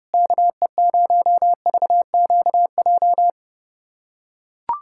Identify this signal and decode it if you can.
Morse code, heard as a keyed tone, about 32 wpm, 700 Hz. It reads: KE0VQJ
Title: USA Callsign DE Callsign 30wpm